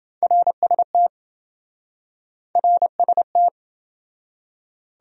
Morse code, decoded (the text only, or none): RST RST